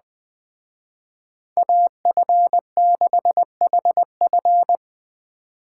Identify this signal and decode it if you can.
Morse code, heard as a keyed tone, about 20 wpm, 700 Hz. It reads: AF6HF
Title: Speed Racing - USA 2x2 Call Signs 17wpm